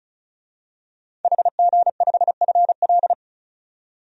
Morse code, heard as a keyed tone, about 34 wpm, 700 Hz. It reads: HG5FL
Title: DXCC-100 Call Signs - 5 Characters - 35wpm